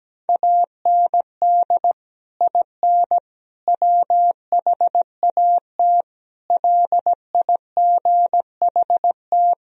Morse code, heard as a keyed tone, about 17 wpm, 700 Hz. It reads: AND IN WHAT LIGHT
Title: Sentences from Top 1000 Words - Review Part 2 - 17wpm